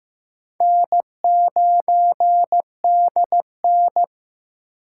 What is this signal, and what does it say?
Morse code, heard as a keyed tone, about 15 wpm, 700 Hz. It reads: N9DN
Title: USA Call Signs - 4 Characters - 6x Character Spacing - 15wpm